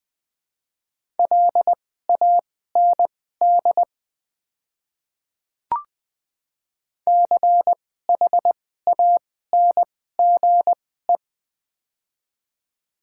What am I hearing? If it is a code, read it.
Morse code, heard as a keyed tone, about 20 wpm, spaced as for 14 wpm, 700 Hz. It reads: LAND CHANGE
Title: 2x Character Spacing - Top 200 Words - 20wpm